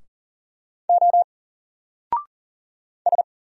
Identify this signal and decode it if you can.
Morse code, heard as a keyed tone, about 39 wpm, 700 Hz. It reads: O S
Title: Single Character (Letter:Number:Punctuation:Prosign) 40wpm